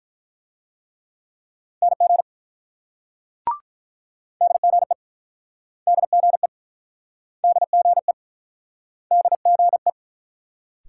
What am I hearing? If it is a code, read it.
Morse code, heard as a keyed tone, about 48 wpm, 700 Hz. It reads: NG DGE DGE DGE DGE